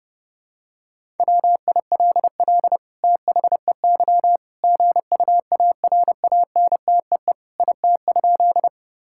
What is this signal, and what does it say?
Morse code, heard as a keyed tone, about 30 wpm, 700 Hz. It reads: WILL THEY GUARANTEE IT?
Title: Sentences from Top 1600 Words 30wpm